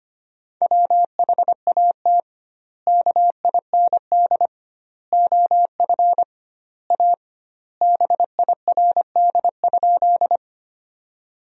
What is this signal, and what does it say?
Morse code, heard as a keyed tone, about 25 wpm, 700 Hz. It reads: WHAT KIND OF A BIRD?
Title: 2x Word-Spacing - Sentences from Top 400 Words - 25wpm